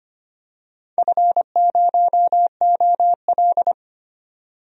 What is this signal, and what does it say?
Morse code, heard as a keyed tone, about 25 wpm, 700 Hz. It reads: F0OL
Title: DXCC-100 Call Signs - 4 Characters - 25wpm